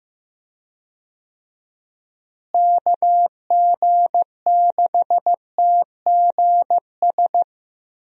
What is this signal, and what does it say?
Morse code, heard as a keyed tone, about 15 wpm, 700 Hz. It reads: KG6TGS
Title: USA Call Signs - 6 Characters - Repeated 2x - 15wpm